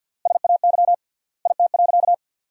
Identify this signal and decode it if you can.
Morse code, heard as a keyed tone, about 50 wpm, 700 Hz. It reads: SAY IT.
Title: 3x Word-Spacing - Sentences from Top 400 Words - 50wpm